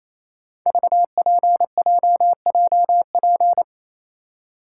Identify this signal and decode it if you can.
Morse code, heard as a keyed tone, about 29 wpm, 700 Hz. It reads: VPJJP
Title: Code Groups - 5 Letters - 28wpm